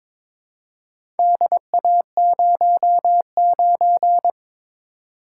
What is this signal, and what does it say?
Morse code, heard as a keyed tone, about 22 wpm, 700 Hz. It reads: DA09
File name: practice / 22wpm / Four Letter-Number Combinations - 22wpm